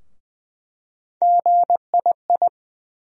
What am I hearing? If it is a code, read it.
Morse code, heard as a keyed tone, about 20 wpm, 700 Hz. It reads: GII
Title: USA Call Sign Postfixes - 3 Letters 20wpm